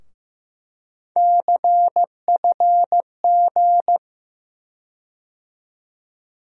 Morse code, heard as a keyed tone, about 15 wpm, 700 Hz. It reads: CFG
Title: USA Call Sign Postfixes - 3 Letters 15wpm